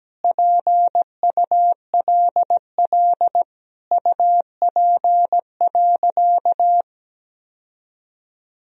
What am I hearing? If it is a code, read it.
Morse code, heard as a keyed tone, about 17 wpm, 700 Hz. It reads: PULL UP.
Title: Sentences from Top 1000 Words - Review Part 2 - 17wpm